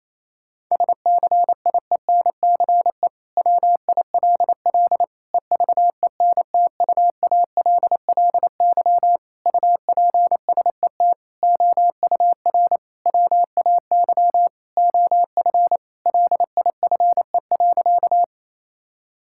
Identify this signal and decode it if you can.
Morse code, heard as a keyed tone, about 28 wpm, 700 Hz. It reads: SCIENCE WILL EVENTUALLY UPSET OUR WAY OF LIFE.
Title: Sentences from Top 1000 Words - Review Part 6 - 28wpm